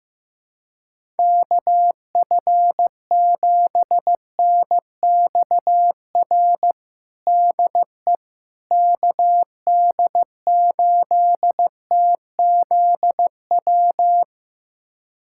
Morse code, heard as a keyed tone, about 16 wpm, 700 Hz. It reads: KF7NXR DE KD8TZW
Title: USA Callsign DE Callsign - 6 Characters - 15wpm